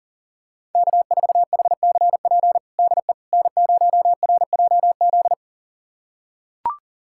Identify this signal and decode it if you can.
Morse code, heard as a keyed tone, about 40 wpm, 700 Hz. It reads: K4HCP DE N0RJZ
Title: USA Callsign DE Callsign - 5 Characters - 40wpm